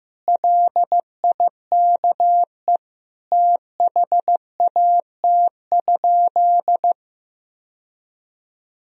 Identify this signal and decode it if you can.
Morse code, heard as a keyed tone, about 15 wpm, 700 Hz. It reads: LIKE THAT?
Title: Sentences from Top 1000 Words - Review Part 2 - 15wpm